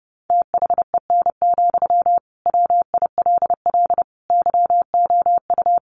Morse code, heard as a keyed tone, about 30 wpm, 700 Hz. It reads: THEN, WILL YOU
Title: Sentences from Top 1500 Words 30wpm